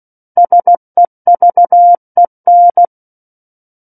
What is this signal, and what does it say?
Morse code, heard as a keyed tone, about 16 wpm, 700 Hz. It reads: SEVEN
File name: Book One / 07-Chapter 7 - 16 WPM